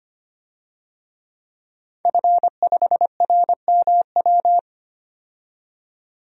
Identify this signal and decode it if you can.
Morse code, heard as a keyed tone, about 25 wpm, 700 Hz. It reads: F5RMW
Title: DXCC-100 Call Signs - 5 Characters - 25wpm